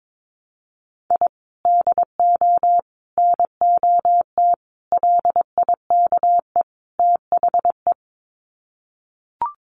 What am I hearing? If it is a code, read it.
Morse code, heard as a keyed tone, about 22 wpm, 700 Hz. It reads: I DO NOT LIKE THE
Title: Sets of 5 Words Encore 22wpm